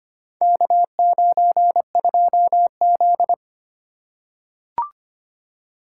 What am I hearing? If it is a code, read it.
Morse code, heard as a keyed tone, about 25 wpm, 700 Hz. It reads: K92Z
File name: practice / 25wpm / Four Letter-Number Combinations - 25wpm